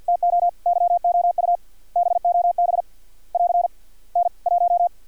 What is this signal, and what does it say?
Morse code, heard as a keyed tone, about 50 wpm, 700 Hz. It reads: TO YOU BOB <AR> N 1
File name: archived / CRSnet